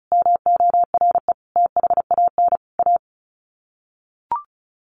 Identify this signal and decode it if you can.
Morse code, heard as a keyed tone, about 34 wpm, 700 Hz. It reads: MORE THAN A
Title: Sets of 4 Words 35wpm